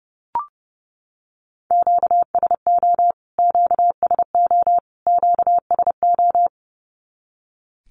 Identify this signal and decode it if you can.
Morse code, heard as a keyed tone, about 30 wpm, 700 Hz. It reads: QSO QSO QSO
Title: Q-Codes Repeated 3x 30wpm